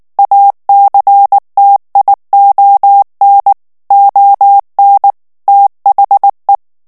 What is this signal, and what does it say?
Morse code, heard as a keyed tone, about 19 wpm, 800 Hz. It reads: ACTION ON THE
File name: ANGLAIS texte clair 3 vitesse 18 mots minute